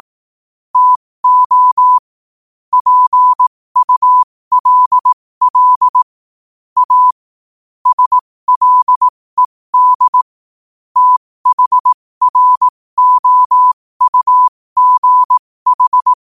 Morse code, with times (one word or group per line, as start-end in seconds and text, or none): 0.8-2.0 TO
2.7-6.0 PULL
6.8-7.1 A
7.9-10.2 SLED
11.0-16.1 THROUGH